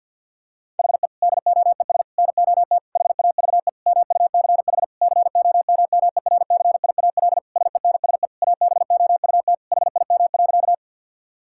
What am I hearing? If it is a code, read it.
Morse code, heard as a keyed tone, about 50 wpm, 700 Hz. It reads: HE DOES NOT HAVE MUCH COMMERCIAL SENSE ABOUT HIM.
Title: Sentences from Top 1000 Words - Review Part 5 - 50wpm